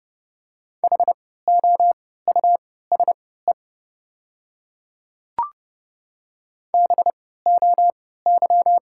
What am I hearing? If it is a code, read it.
Morse code, heard as a keyed tone, about 30 wpm, spaced as for 17 wpm, 700 Hz. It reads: HOUSE BOY
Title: 3x Character Spacing - Top 200 Words - 30wpm